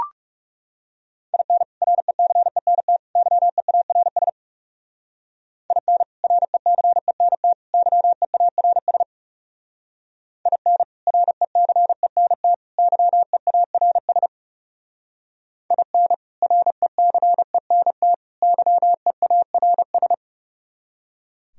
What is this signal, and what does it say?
Morse code, heard as a keyed tone, about 37 wpm, 700 Hz. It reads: IN RECENT YEARS IN RECENT YEARS IN RECENT YEARS IN RECENT YEARS
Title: Speed-Racing - Sets of 3 Words 30wpm